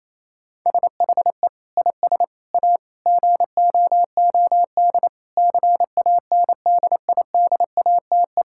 Morse code, heard as a keyed tone, about 28 wpm, 700 Hz. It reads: SHE IS A GOOD CANDIDATE
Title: Sentences from Top 1600 Words 28wpm